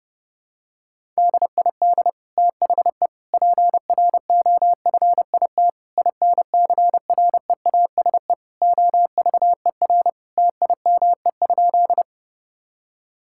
Morse code, heard as a keyed tone, about 30 wpm, 700 Hz. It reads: DID THE PROFIT INCREASE OVER TIME?
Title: Sentences from Top 1000 Words - Review Part 5 - 30wpm